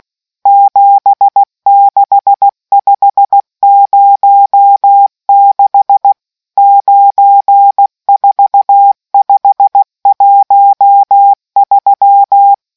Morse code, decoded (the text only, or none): 76506 94513